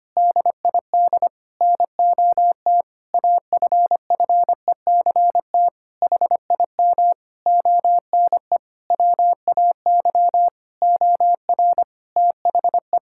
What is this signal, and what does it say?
Morse code, heard as a keyed tone, about 25 wpm, 700 Hz. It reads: DID NOT AFFECT HIM ONE WAY OR THE
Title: Sentences from Top 1000 Words - Review Part 6 - 25wpm